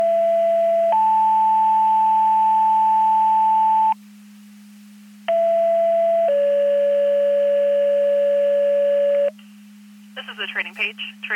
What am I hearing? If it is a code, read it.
Two-tone page, heard as a one-second tone, then a three-second tone, about 670 then 880 Hz, about 670 then 570 Hz, with speech. It is heard from 0.0 to 9.3 s.